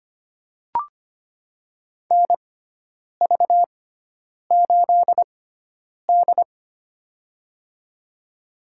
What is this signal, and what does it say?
Morse code, heard as a keyed tone, about 25 wpm, spaced as for 9 wpm, 700 Hz. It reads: NV8D
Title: USA Call Signs - 4 Characters - 6x Character Spacing - 25wpm